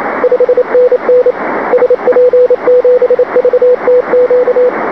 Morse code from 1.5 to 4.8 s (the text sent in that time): SP7VTQ